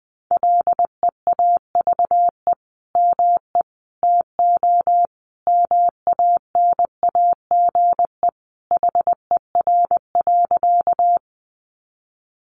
Morse code, heard as a keyed tone, about 20 wpm, 700 Hz. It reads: LEAVE ME TO MANAGE HER.
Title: Sentences from Top 1400 Words 20wpm